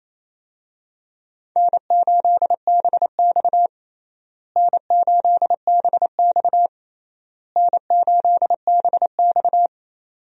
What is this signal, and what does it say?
Morse code, heard as a keyed tone, about 28 wpm, 700 Hz. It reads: N8BX N8BX N8BX
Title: USA Call Signs - 4 Characters - Repeated 3x - 28wpm